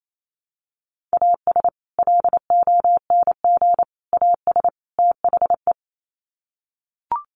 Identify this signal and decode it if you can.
Morse code, heard as a keyed tone, about 28 wpm, 700 Hz. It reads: AS LONG AS THE
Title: Sets of 4 Words 28wpm